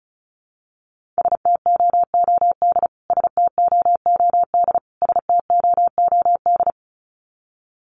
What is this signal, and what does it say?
Morse code, heard as a keyed tone, about 34 wpm, 700 Hz. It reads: STOOD STOOD STOOD
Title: Top 500 Words Repeated 35wpm